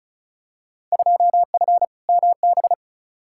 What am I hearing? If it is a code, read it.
Morse code, heard as a keyed tone, about 36 wpm, 700 Hz. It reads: 2F MB
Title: Field Day Exchanges Encore - 35wpm